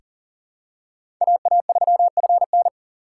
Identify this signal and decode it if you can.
Morse code, heard as a keyed tone, about 40 wpm, 700 Hz. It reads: AA3FN
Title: USA Call Signs - 4x Character Spacing - 40wpm